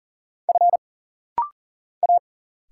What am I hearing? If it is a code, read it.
Morse code, heard as a keyed tone, about 40 wpm, 700 Hz. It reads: F A
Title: Single Letters - Rapid-Fire 40wpm